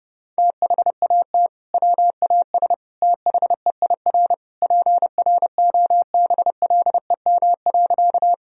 Morse code, heard as a keyed tone, about 32 wpm, 700 Hz. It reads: THAT WAS THEIR PROBLEM.
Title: Sentences from Top 1000 Words - Review Part 2 - 30wpm